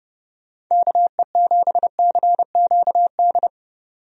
Morse code, heard as a keyed tone, about 31 wpm, 700 Hz. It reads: KE7CQD
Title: USA Call Signs - 6 Characters - 30wpm